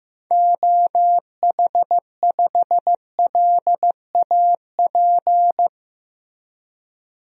Morse code, heard as a keyed tone, about 15 wpm, 700 Hz. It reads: OH5LAP
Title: DXCC-100 Call Signs - 6 Characters - Repeated 3x - 15wpm